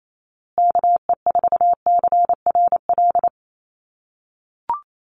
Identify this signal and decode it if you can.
Morse code, heard as a keyed tone, about 28 wpm, 700 Hz. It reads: KE4CRL
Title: USA Call Signs Repeated 3x 28wpm